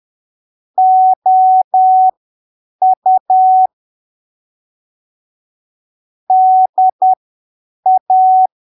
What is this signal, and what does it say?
Morse code, heard as a keyed tone, about 10 wpm, spaced as for 7 wpm, 750 Hz. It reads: OU DA
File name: arquivos / texto / texto 04 10ppm